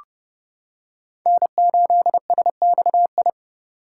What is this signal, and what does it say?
Morse code, heard as a keyed tone, about 30 wpm, 700 Hz. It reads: N8SXI